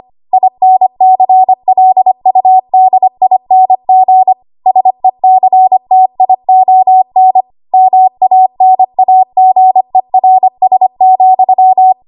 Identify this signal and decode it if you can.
Morse code, heard as a keyed tone, about 25 wpm, 750 Hz. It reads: INCLUDING SECTION MANAGERS,
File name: Archive / 25 WPM / 25WPM